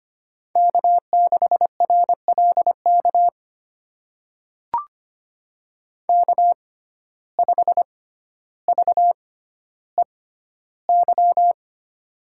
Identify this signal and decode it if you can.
Morse code, heard as a keyed tone, about 25 wpm, spaced as for 13 wpm, 700 Hz. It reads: K6RLK K5VEY
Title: USA Call Signs - 5 Characters - 6x Character Spacing - 25wpm